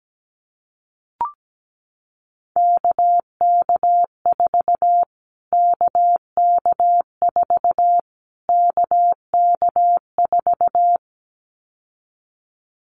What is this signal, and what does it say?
Morse code, heard as a keyed tone, about 17 wpm, 700 Hz. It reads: KK4 KK4 KK4
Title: Callsign Prefixes for Top 25 DXCC - Repeated 3x - 17wpm